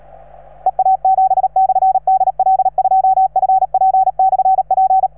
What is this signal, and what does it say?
Morse code, heard as a keyed tone, about 37 wpm, 760 Hz. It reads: EA7/DL2FP/P